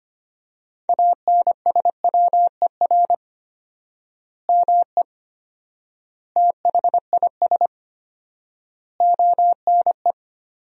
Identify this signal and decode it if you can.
Morse code, heard as a keyed tone, about 25 wpm, 700 Hz. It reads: ANSWER ME THIS ONE
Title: 4x Word-Spacing - Sentences from Top 400 Words - 25wpm